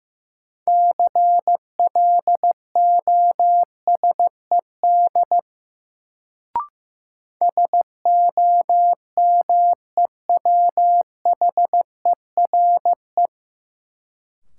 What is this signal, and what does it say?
Morse code, heard as a keyed tone, about 15 wpm, 700 Hz. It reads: CLOSED SOMEWHERE
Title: Sentences from Top 1400 Words - New Word Drill - 15wpm